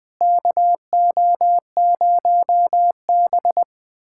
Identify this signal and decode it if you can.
Morse code, heard as a keyed tone, about 21 wpm, 700 Hz. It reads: KO0B
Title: USA Callsign DE Callsign - 4 Characters - 20wpm